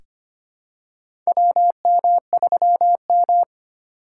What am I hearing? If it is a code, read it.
Morse code, heard as a keyed tone, about 25 wpm, 700 Hz. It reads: WM3M
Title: USA Call Signs - 4x Character Spacing - 25wpm